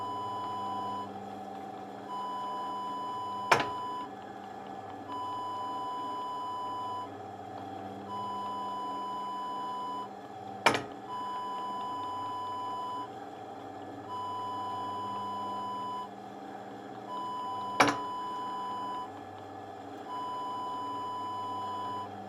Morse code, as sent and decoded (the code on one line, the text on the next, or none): none